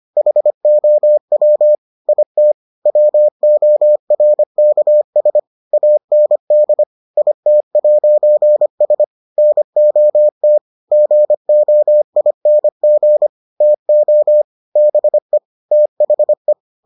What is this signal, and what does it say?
Morse code, heard as a keyed tone, about 25 wpm, 600 Hz. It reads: HOW IT WORKS AND IT'S NOT GOING TO BE THE